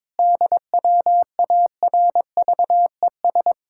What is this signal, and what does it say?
Morse code, heard as a keyed tone, about 22 wpm, 700 Hz. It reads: DWARVES
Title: Pangram Challenge v2 - 22wpm